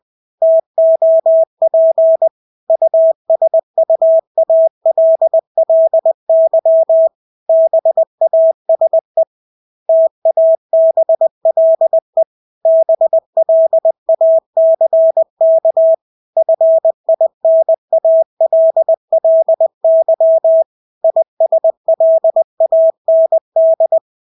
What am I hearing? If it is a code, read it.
Morse code, heard as a keyed tone, about 20 wpm, 650 Hz. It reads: TOP USUALLY BASE TABLE BLACK FINALLY ISLAND
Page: Engelske ord 20 wpm